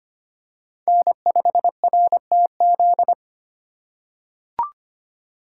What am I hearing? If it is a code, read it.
Morse code, heard as a keyed tone, about 25 wpm, 700 Hz. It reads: N5RTZ
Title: USA Call Signs - 5 Characters - 2x Character Spacing - 25wpm